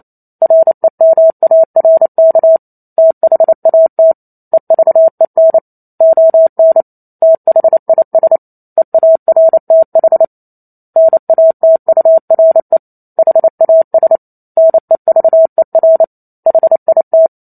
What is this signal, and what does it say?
Morse code, heard as a keyed tone, about 28 wpm, 650 Hz. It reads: REMARK THAT EVEN ON THIS EARTH NATURE HAS NEVER HIT
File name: Chapter 2 - 28 WPM